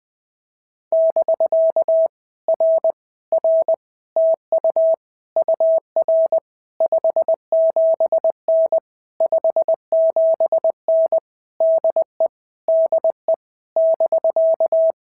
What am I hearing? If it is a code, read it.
Morse code, heard as a keyed tone, about 20 wpm, 650 Hz. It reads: <BK> R R TU UR 57N 57N DE DE <BK>
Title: POTA QSO Practice - 20wpm